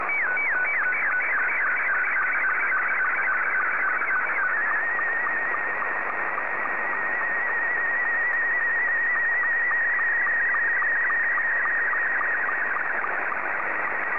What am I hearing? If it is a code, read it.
SSTV (slow-scan television), Scottie 2.
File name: sstv / S2-VK3ADZ